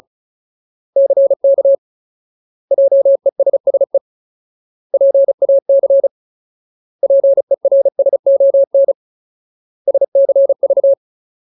Morse code, heard as a keyed tone, about 34 wpm, 550 Hz. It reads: CK JESSE PAC PERSON SCV
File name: Ultimate Word List / 35WPM Ultimate